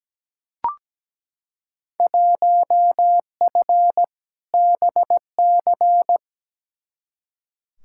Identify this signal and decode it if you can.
Morse code, heard as a keyed tone, about 17 wpm, 700 Hz. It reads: 1F BC
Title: Field Day Exchanges Encore - 17wpm